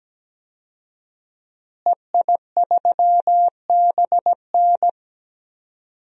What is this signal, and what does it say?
Morse code, heard as a keyed tone, about 17 wpm, 700 Hz. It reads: EI3BN
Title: DXCC-100 Call Signs - 5 Characters - Repeated 3x - 17wpm